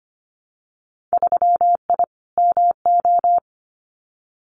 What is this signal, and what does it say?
Morse code, heard as a keyed tone, about 25 wpm, 700 Hz. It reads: 3I MO
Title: Winter Field Day Exchanges 25wpm